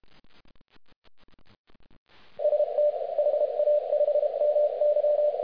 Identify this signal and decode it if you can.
Morse code, heard as a keyed tone, about 33 wpm, 590 Hz, with radio noise. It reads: ST HA5MY